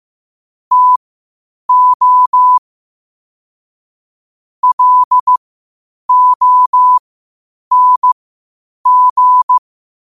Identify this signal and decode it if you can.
Morse code, heard as a keyed tone, about 15 wpm, spaced as for 8 wpm, 1000 Hz. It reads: TO LONG